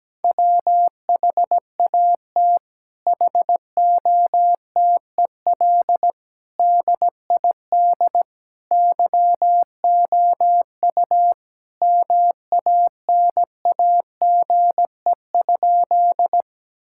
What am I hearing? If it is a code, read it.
Morse code, heard as a keyed tone, about 17 wpm, 700 Hz. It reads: WHAT HOTEL DID YOU MANAGE?
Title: Sentences from Top 1600 Words 17wpm